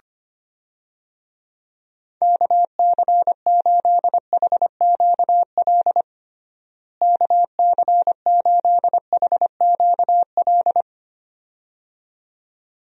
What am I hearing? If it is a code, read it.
Morse code, heard as a keyed tone, about 25 wpm, 700 Hz. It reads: KC8HQL KC8HQL
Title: USA Call Signs - 6 Characters - Repeated 2x - 25wpm